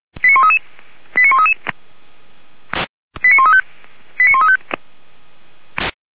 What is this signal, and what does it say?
Five-tone selective call (ZVEI), heard as quick runs of five tones, tones of about 70 ms. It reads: 89133 89133 89135 89135